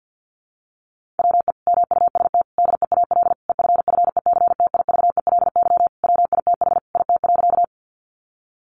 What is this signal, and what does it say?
Morse code, heard as a keyed tone, about 50 wpm, 700 Hz. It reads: WE MUST DEAL EFFECTIVELY WITH IT.
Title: Sentences from Top 1400 Words 50wpm